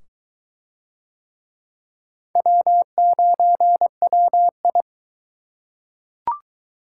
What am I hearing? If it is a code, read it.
Morse code, heard as a keyed tone, about 22 wpm, 700 Hz. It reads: W9WI